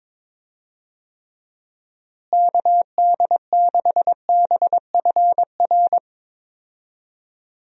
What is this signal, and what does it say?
Morse code, heard as a keyed tone, about 22 wpm, 700 Hz. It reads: KD6BFR
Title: USA Call Signs - 6 Characters - Repeated 2x - 22wpm